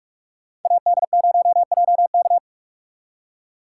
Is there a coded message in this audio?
Yes: Morse code from 0.7 to 2.4 s, heard as a keyed tone, about 44 wpm, 700 Hz. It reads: AD0JK